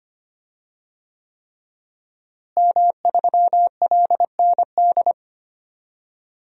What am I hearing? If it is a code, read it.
Morse code, heard as a keyed tone, about 25 wpm, 700 Hz. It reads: M3LND
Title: DXCC-100 Call Signs - 5 Characters - Repeated 3x - 25wpm